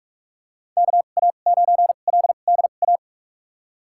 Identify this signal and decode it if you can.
Morse code, heard as a keyed tone, about 47 wpm, spaced as for 33 wpm, 700 Hz. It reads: KA9LDA
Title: USA Call Signs - 6 Characters - 2x Character Spacing - 45wpm